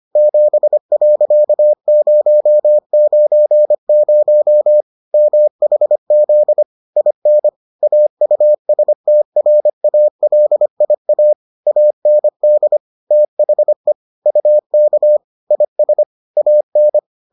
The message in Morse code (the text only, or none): 7.090 MHZ IN AUSTRALIA AND THE UK IS AN